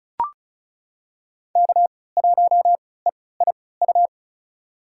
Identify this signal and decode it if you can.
Morse code, heard as a keyed tone, about 36 wpm, spaced as for 20 wpm, 700 Hz. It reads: K1EIU
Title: USA Call Signs - 5 Characters - 3x Character Spacing - 35wpm